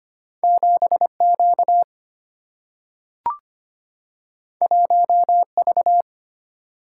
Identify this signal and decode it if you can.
Morse code, heard as a keyed tone, about 25 wpm, 700 Hz. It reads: 7Q 1V
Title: Single-Letter Single-Number 25wpm